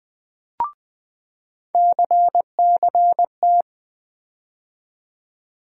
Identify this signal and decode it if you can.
Morse code, heard as a keyed tone, about 20 wpm, 700 Hz. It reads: CCT
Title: USA Call Sign Postfixes - 3 Letters 20wpm